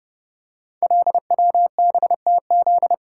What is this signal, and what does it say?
Morse code, heard as a keyed tone, about 30 wpm, 700 Hz. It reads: LWBTZ
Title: Code Groups - 5 Letter:Numbers - 30wpm